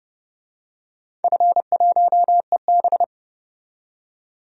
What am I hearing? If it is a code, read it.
Morse code, heard as a keyed tone, about 32 wpm, 700 Hz. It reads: F1EB
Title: DXCC-100 Call Signs - 4 Characters - 30wpm